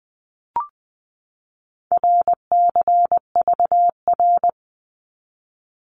Morse code, heard as a keyed tone, about 20 wpm, 700 Hz. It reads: RCVR
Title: Abbreviations 20wpm